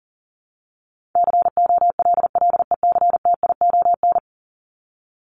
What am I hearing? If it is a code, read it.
Morse code, heard as a keyed tone, about 38 wpm, 700 Hz. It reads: COLLECTION
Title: Top 1001-1250 Words 40wpm